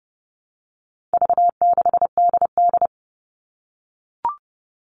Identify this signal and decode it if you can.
Morse code, heard as a keyed tone, about 30 wpm, 700 Hz. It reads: V6DD
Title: All Cut Numbers - 4-Digit 30wpm